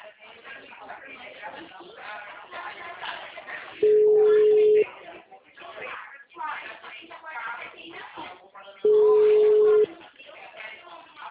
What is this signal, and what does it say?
Ringback tone (Europe, 425 Hz), with speech.